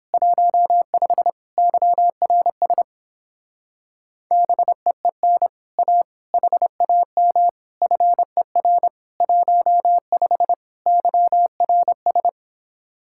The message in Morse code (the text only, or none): 15 YRS BEEN A HAM FER 15 YRS